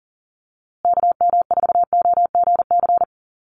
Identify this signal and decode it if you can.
Morse code, heard as a keyed tone, about 40 wpm, 700 Hz. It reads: KM4OGC
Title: USA Call Signs Repeated 3x 40wpm